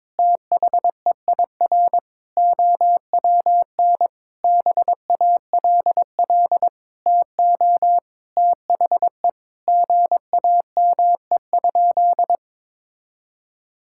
Morse code, heard as a keyed tone, about 22 wpm, 700 Hz. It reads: THEIR OWN BALL TO THE GAME?
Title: Sentences from Top 1000 Words - Review Part 6 - 22wpm